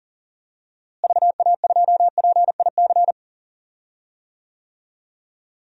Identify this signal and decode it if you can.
Morse code, heard as a keyed tone, about 37 wpm, 700 Hz. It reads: VA2PIC
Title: DXCC-100 Call Signs - 6 Characters - Repeated 3x - 40wpm